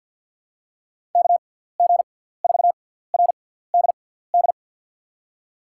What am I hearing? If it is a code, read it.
Morse code, heard as a keyed tone, about 53 wpm, spaced as for 17 wpm, 700 Hz. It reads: KG4RDD
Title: USA Call Signs - 6 Characters - 6x Character Spacing - 50wpm